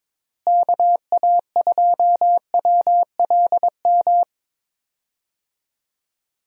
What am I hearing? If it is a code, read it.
Morse code, heard as a keyed tone, about 22 wpm, 700 Hz. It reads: KA2WLM
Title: USA Call Signs - 6 Characters - Repeated 3x - 22wpm